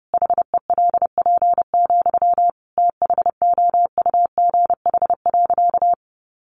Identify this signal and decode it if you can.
Morse code, heard as a keyed tone, about 30 wpm, 700 Hz. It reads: HELP, THOUGH.
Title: Sentences from Top 1100 Words 30wpm